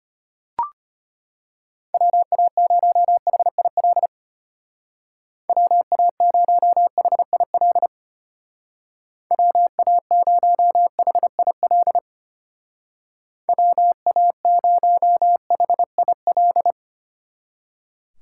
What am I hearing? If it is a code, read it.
Morse code, heard as a keyed tone, about 32 wpm, 700 Hz. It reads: WA0HIL WA0HIL WA0HIL WA0HIL